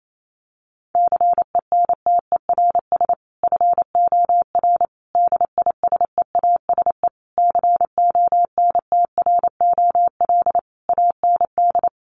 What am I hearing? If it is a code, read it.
Morse code, heard as a keyed tone, about 28 wpm, 700 Hz. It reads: CENTERS FOR DISEASE CONTROL AND
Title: Sets of 5 Words Encore2 28wpm